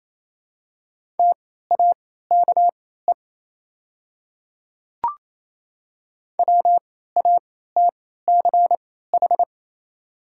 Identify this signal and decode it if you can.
Morse code, heard as a keyed tone, about 27 wpm, spaced as for 16 wpm, 690 Hz. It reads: TAKE WATCH
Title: 3x Character Spacing - Top 500 Words - 28wpm